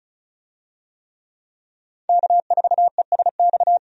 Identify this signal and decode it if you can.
Morse code, heard as a keyed tone, about 34 wpm, 700 Hz. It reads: K4ESX
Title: USA Call Signs - 5 Characters - 35wpm